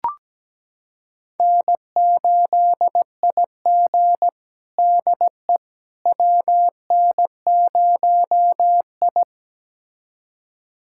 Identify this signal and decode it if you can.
Morse code, heard as a keyed tone, about 17 wpm, 700 Hz. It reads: N8IG DE WN0I
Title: USA Callsign DE Callsign - 4 Characters - 17wpm